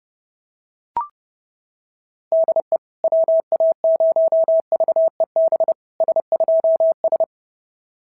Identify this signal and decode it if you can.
Morse code, heard as a keyed tone, about 28 wpm, 650 Hz. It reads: DE WA0VEB S2S
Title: SOTA QSO Practice 30wpm